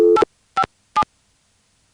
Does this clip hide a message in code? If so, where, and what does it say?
Yes: DTMF from 0.2 to 1.0 s, heard as telephone touch tones, tones of about 70 ms. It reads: 857